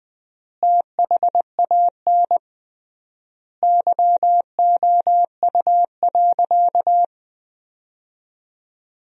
Morse code, heard as a keyed tone, about 21 wpm, 700 Hz. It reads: THAN YOU.
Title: 3x Word-Spacing - Sentences from Top 400 Words - 20wpm